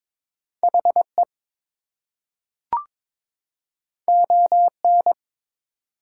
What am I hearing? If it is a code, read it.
Morse code, heard as a keyed tone, about 22 wpm, 700 Hz. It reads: HE ON
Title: Two Letter Words - 22wpm